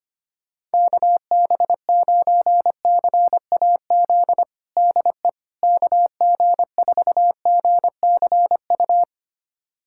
Morse code, heard as a keyed tone, about 25 wpm, 700 Hz. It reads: KB9CAZ DE KG4GCU
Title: USA Callsign DE Callsign 25wpm